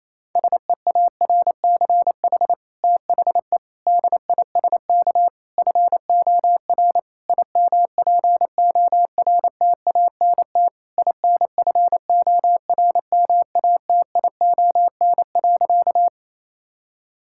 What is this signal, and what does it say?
Morse code, heard as a keyed tone, about 28 wpm, 700 Hz. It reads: SEARCH THE DISK FOR IMPORTANT INFORMATION.
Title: Sentences from Top 1000 Words - Review Part 6 - 28wpm